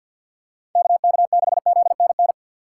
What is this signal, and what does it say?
Morse code, heard as a keyed tone, about 46 wpm, 700 Hz. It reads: KK6ZNN
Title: USA Call Signs - 6 Characters - 4x Character Spacing - 50wpm